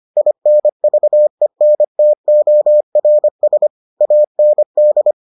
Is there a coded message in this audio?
Yes: Morse code from 0.2 to 5.1 s, heard as a keyed tone, about 25 wpm, 600 Hz. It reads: INVENTORS AND